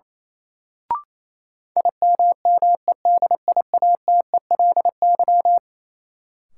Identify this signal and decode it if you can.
Morse code, heard as a keyed tone, about 28 wpm, 700 Hz. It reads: IMMEDIATELY
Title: Sentences from Top 1100 Words - New Word Drill Alt - 28wpm